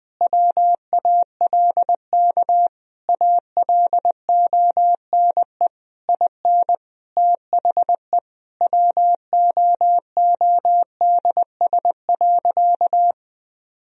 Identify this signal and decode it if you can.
Morse code, heard as a keyed tone, about 21 wpm, 700 Hz. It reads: WALK ALONE IN THE WOODS.
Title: Sentences from Top 1000 Words - Review Part 2 - 20wpm